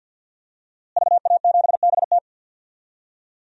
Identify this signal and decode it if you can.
Morse code, heard as a keyed tone, about 48 wpm, 700 Hz. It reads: VA7BT
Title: DXCC-100 Call Signs - 5 Characters - Repeated 2x - 50wpm